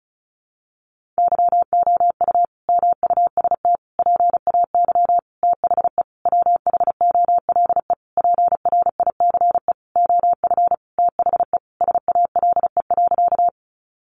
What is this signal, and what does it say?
Morse code, heard as a keyed tone, about 34 wpm, 700 Hz. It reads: YOU MUST PAY THE WHOLE PRICE OF THE SALE.
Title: Sentences from Top 1400 Words 35wpm